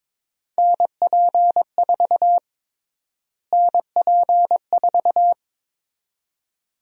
Morse code, heard as a keyed tone, about 22 wpm, 700 Hz. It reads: NP4 NP4
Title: USA Call Sign Prefixes - Repeated 2x - 22wpm